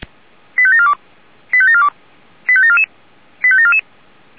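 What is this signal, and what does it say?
Five-tone selective call (ZVEI), heard as quick runs of five tones, tones of about 70 ms. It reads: 85742 85742 85744 85744
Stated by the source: ZVEI five-tone selective call 89485 received on an SDR.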